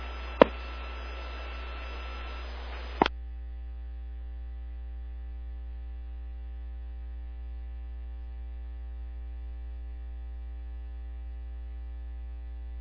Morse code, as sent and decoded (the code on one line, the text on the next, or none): none